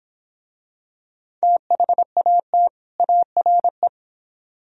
Morse code, heard as a keyed tone, about 26 wpm, 700 Hz. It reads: THAT ARE